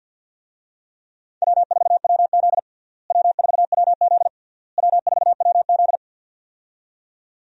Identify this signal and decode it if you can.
Morse code, heard as a keyed tone, about 46 wpm, 700 Hz. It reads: W4WZ W4WZ W4WZ
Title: USA Call Signs - 4 Characters - Repeated 3x - 50wpm